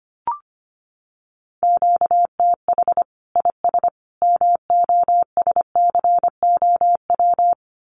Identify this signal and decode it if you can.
Morse code, heard as a keyed tone, about 25 wpm, 700 Hz. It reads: QTH IS MOSCOW
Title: Common QSO Phrases 25wpm